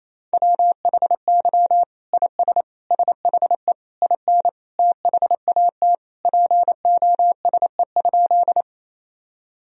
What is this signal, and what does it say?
Morse code, heard as a keyed tone, about 28 wpm, 700 Hz. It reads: WHY IS SHE IN THAT POSE?
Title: Sentences from Top 1000 Words - Review Part 2 - 28wpm